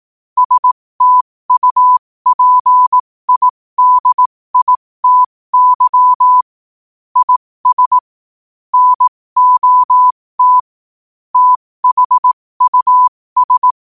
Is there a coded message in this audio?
Yes: Morse code from 0.4 to 13.7 s, heard as a keyed tone, about 18 wpm, spaced as for 15 wpm, 1000 Hz. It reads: STUPIDITY IS NOT THUS